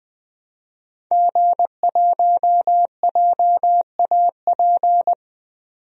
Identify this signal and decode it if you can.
Morse code, heard as a keyed tone, about 20 wpm, 700 Hz. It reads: G1JAP
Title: DXCC-100 Call Signs - 5 Characters - 20wpm